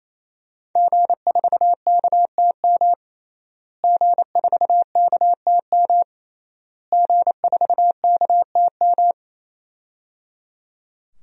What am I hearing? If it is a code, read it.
Morse code, heard as a keyed tone, about 27 wpm, 700 Hz. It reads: G4KTM G4KTM G4KTM
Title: DXCC-100 Call Signs - 5 Characters - Repeated 3x - 28wpm